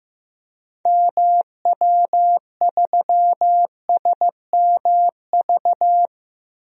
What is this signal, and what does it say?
Morse code, heard as a keyed tone, about 15 wpm, 700 Hz. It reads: MW3SMV
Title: DXCC-100 Call Signs - 6 Characters - Repeated 2x - 15wpm